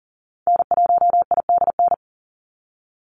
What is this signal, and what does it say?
Morse code, heard as a keyed tone, about 40 wpm, 700 Hz. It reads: N1IDN
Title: NA SOTA Activator Callsigns 40wpm